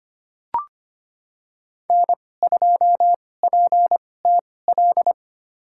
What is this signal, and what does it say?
Morse code, heard as a keyed tone, about 25 wpm, spaced as for 18 wpm, 700 Hz. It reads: N2PTL
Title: USA Call Signs - 2x Character Spacing - 25wpm